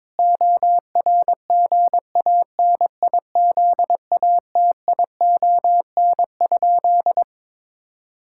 Morse code, heard as a keyed tone, about 22 wpm, 700 Hz. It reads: ORGANIZATION?
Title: Sentences from Top 1000 Words - Review Part 5 - 22wpm